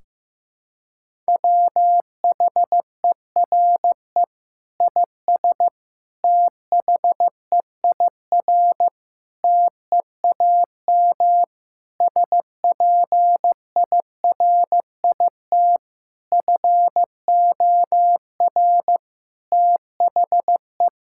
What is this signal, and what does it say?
Morse code, heard as a keyed tone, about 15 wpm, 700 Hz. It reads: WHERE IS THEIR TEAM SPIRIT FOR THE
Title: Sentences from Top 1600 Words 15wpm